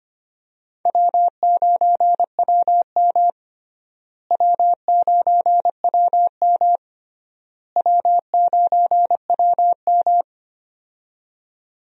Morse code, heard as a keyed tone, about 25 wpm, 700 Hz. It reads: W9WM W9WM W9WM
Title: USA Call Signs - 4 Characters - Repeated 3x - 25wpm